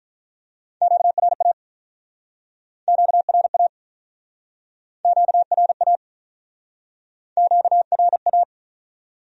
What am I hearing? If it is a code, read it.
Morse code, heard as a keyed tone, about 46 wpm, 700 Hz. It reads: QRA QRA QRA QRA